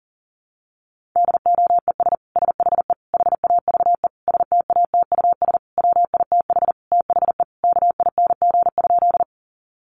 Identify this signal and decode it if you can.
Morse code, heard as a keyed tone, about 40 wpm, 700 Hz. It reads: DOES SHE HAVE STATUS WITH THE KING?
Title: Sentences from Top 1400 Words 40wpm